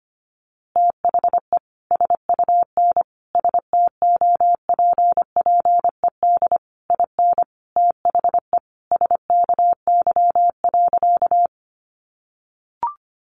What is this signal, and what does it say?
Morse code, heard as a keyed tone, about 25 wpm, 700 Hz. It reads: THE SUN STOPPED IN THE SKY.
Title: A Sound Of Thunder 25wpm